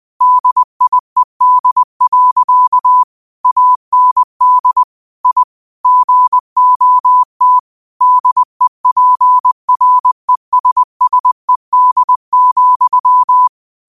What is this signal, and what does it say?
Morse code, heard as a keyed tone, about 20 wpm, 1000 Hz. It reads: DIED. AND I GOT DEPRESSED,
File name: qotdfiles / 20wpm